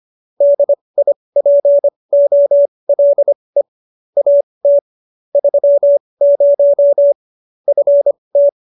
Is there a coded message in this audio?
Yes: Morse code from 0.4 to 8.5 s, heard as a keyed tone, about 25 wpm, spaced as for 20 wpm, 570 Hz. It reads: DIPOLE AT 30 FT